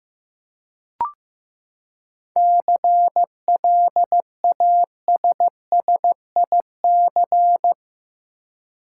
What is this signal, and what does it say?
Morse code, heard as a keyed tone, about 15 wpm, 700 Hz. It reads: CLASSIC
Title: Seven Letter Words - 15wpm